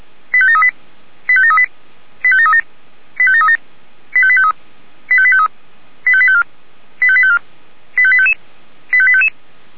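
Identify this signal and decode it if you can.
Five-tone selective call (ZVEI), heard as quick runs of five tones, tones of about 70 ms. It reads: 85738 85738 85737 85737 85853 85853 85854 85854 85855 85855
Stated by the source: ZVEI five-tone selective call received by SDR.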